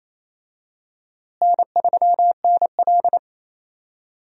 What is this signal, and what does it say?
Morse code, heard as a keyed tone, about 28 wpm, 700 Hz. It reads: N3NL
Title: USA Call Signs - 4 Characters - Repeated 2x - 28wpm